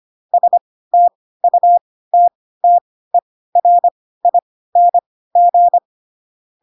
Morse code, heard as a keyed tone, about 25 wpm, spaced as for 16 wpm, 700 Hz. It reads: STUTTERING
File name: Pinocchio / Pinocchio ch1 10 25 ik1hge